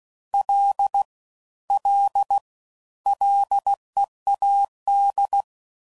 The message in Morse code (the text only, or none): L L LEAD